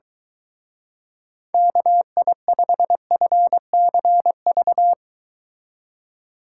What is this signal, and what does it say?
Morse code, heard as a keyed tone, about 23 wpm, 700 Hz. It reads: KI5FCV